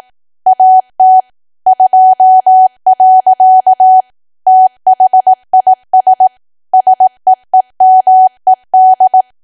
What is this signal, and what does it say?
Morse code, heard as a keyed tone, about 18 wpm, 750 Hz. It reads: AT 2. THIS SEEMED
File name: cwa / W1AW-Level2 / 18WPM